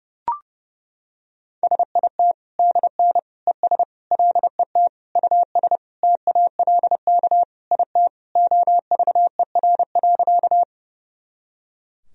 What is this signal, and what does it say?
Morse code, heard as a keyed tone, about 30 wpm, 700 Hz. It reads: SIT DN ES LET US TALK IT OVER.
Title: Sentences from Top 500 Words and Abbreviations 30wpm